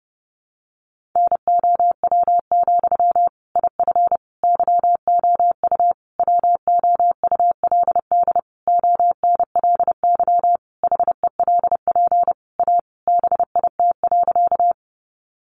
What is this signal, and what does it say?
Morse code, heard as a keyed tone, about 28 wpm, 700 Hz. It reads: NOW, IF YOU WOULD ONLY HELP A BIT.
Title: Sentences from Top 900 Words 30wpm